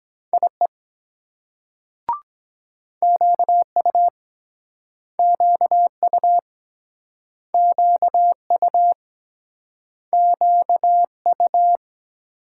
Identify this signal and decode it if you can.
Morse code, heard as a keyed tone, about 22 wpm, 690 Hz. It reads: IE QU QU QU QU